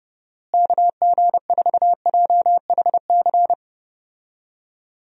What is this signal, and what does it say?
Morse code, heard as a keyed tone, about 28 wpm, 700 Hz. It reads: KG4JHC
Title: USA Call Signs - 6 Characters - Repeated 2x - 30wpm